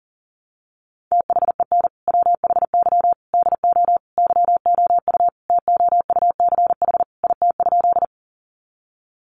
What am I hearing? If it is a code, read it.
Morse code, heard as a keyed tone, about 40 wpm, 700 Hz. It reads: THEN WHY DO YOU TOUCH IT?
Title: Sentences from Top 1200 Words 40wpm